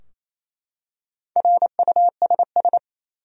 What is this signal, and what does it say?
Morse code, heard as a keyed tone, about 27 wpm, 700 Hz. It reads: RUSS
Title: USA Common Names - Max 4 Letters - 28wpm